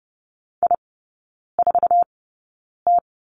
Morse code, heard as a keyed tone, about 30 wpm, spaced as for 9 wpm, 700 Hz. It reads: I4T
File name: Lesson 18e - Letter m - Three Characters 30wpm